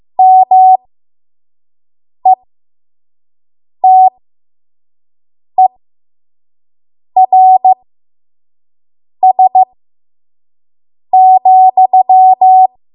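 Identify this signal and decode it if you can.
Morse code, heard as a keyed tone, about 15 wpm, spaced as for 5 wpm, 750 Hz. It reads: METERS,
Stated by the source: 5 WPM Morse code practice.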